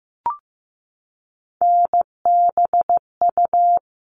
Morse code, heard as a keyed tone, about 15 wpm, 700 Hz. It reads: NBU
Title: All Cut Numbers - 3-Digit 15wpm